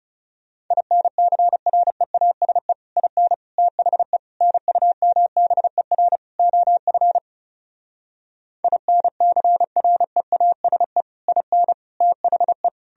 Morse code, heard as a keyed tone, about 40 wpm, 700 Hz. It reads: INCREASE IN THE NUMBER OF INCREASE IN THE